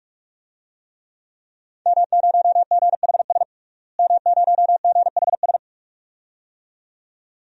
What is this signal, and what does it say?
Morse code, heard as a keyed tone, about 46 wpm, 700 Hz. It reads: M0GHS M0GHS
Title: DXCC-100 Call Signs - 5 Characters - Repeated 2x - 45wpm